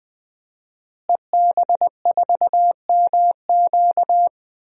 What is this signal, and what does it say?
Morse code, heard as a keyed tone, about 20 wpm, 700 Hz. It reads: EB4MQ
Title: DXCC-100 Call Signs - 5 Characters - Repeated 3x - 20wpm